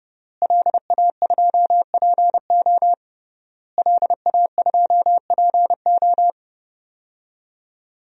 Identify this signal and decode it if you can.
Morse code, heard as a keyed tone, about 30 wpm, 700 Hz. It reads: LA2PO LA2PO
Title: DXCC-100 Call Signs - 5 Characters - Repeated 3x - 30wpm